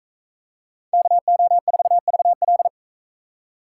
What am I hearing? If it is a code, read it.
Morse code, heard as a keyed tone, about 41 wpm, 700 Hz. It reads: KO4VL